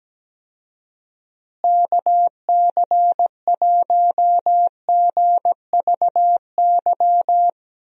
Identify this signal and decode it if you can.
Morse code, heard as a keyed tone, about 17 wpm, 700 Hz. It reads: KC1GVY
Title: USA Call Signs - 6 Characters - Repeated 2x - 17wpm